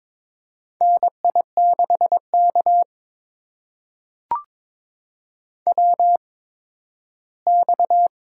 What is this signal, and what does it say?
Morse code, heard as a keyed tone, about 22 wpm, spaced as for 13 wpm, 700 Hz. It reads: NI6K WX
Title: USA Call Signs - 4 Characters - 8x Character Spacing - 22wpm